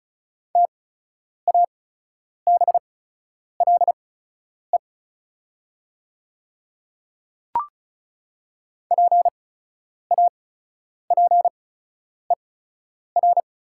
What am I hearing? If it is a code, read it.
Morse code, heard as a keyed tone, about 34 wpm, spaced as for 10 wpm, 700 Hz. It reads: TABLE PAPER
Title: 8x Character Spacing - Top 500 Words - 35wpm